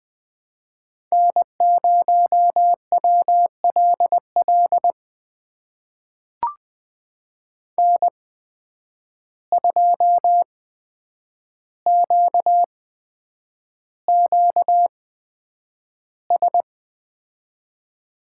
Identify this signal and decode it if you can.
Morse code, heard as a keyed tone, about 20 wpm, spaced as for 9 wpm, 700 Hz. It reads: N0WLL N2QQS
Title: USA Call Signs - 5 Characters - 8x Character Spacing - 20wpm